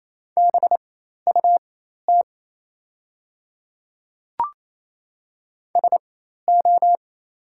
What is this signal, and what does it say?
Morse code, heard as a keyed tone, about 28 wpm, spaced as for 13 wpm, 700 Hz. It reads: BUT SO
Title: 4x Character Spacing - Top 100 Words - 28wpm